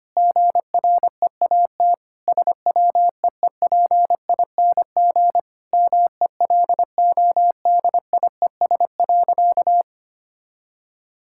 Morse code, heard as a keyed tone, about 25 wpm, 700 Hz. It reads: GREAT SWEEPING MELODIES.
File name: practice / 25wpm / All-Age Appropriate Jokes - 25wpm